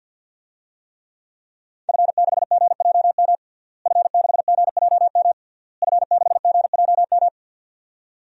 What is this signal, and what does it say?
Morse code, heard as a keyed tone, about 50 wpm, 700 Hz. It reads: F6GJM F6GJM F6GJM
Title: DXCC-100 Call Signs - 5 Characters - Repeated 3x - 50wpm